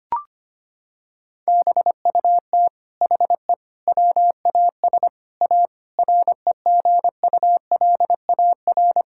Morse code, heard as a keyed tone, about 25 wpm, 700 Hz. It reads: BUT HE WAS A REGULAR
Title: Sentences from Top 1000 Words - Review Part 6 - 25wpm